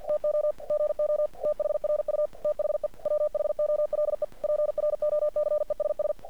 Morse code, heard as a keyed tone, about 50 wpm, 600 Hz. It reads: TO GO THRU THE WHOLE PROCESS
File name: archived / CRSnet